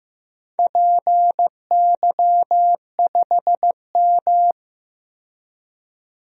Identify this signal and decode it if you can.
Morse code, heard as a keyed tone, about 15 wpm, 700 Hz. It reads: PY5M
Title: DXCC-100 Call Signs - 4 Characters - 15wpm